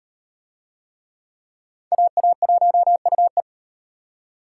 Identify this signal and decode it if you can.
Morse code, heard as a keyed tone, about 34 wpm, 700 Hz. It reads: AA1UE